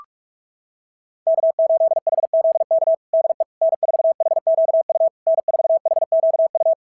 Morse code, heard as a keyed tone, about 47 wpm, 660 Hz. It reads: K8HZK DE N4HQU N4HQU
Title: SOTA QSO Practice 45wpm